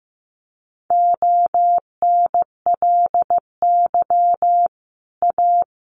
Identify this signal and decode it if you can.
Morse code, heard as a keyed tone, about 15 wpm, 700 Hz. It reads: ONLY A
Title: Sets of 5 Words Encore 15wpm